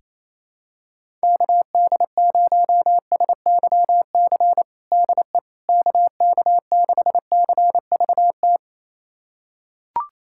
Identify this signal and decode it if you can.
Morse code, heard as a keyed tone, about 28 wpm, 700 Hz. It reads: KD0SYC DE KK6CVT
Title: USA Callsign DE Callsign 28wpm